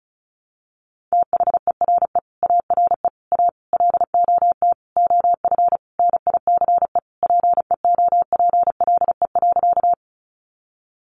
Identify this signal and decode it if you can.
Morse code, heard as a keyed tone, about 34 wpm, 700 Hz. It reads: THERE ARE A LOT OF NICE PEOPLE.
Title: Sentences from Top 1100 Words 35wpm